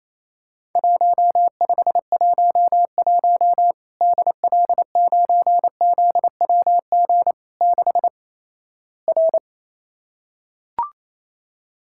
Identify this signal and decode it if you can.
Morse code, heard as a keyed tone, about 28 wpm, 700 Hz. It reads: 1511 DL9ZWG 6 R
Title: Worked All Europe DX Contest - QTC Exchange - 28wpm